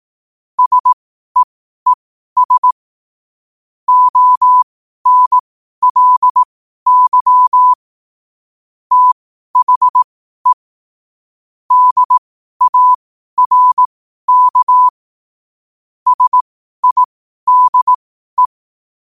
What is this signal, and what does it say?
Morse code, heard as a keyed tone, about 18 wpm, spaced as for 12 wpm, 1000 Hz. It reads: SEES ONLY THE DARK SIDE
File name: qotdfiles / 12wpm